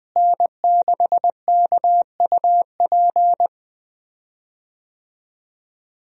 Morse code, heard as a keyed tone, about 20 wpm, 700 Hz. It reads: N6KUP
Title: USA Call Signs - 5 Characters - Repeated 3x - 20wpm